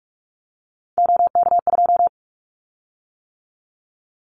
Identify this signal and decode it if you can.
Morse code, heard as a keyed tone, about 44 wpm, 700 Hz. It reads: OK2
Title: Callsign Prefixes for Top 100 DXCC - 45wpm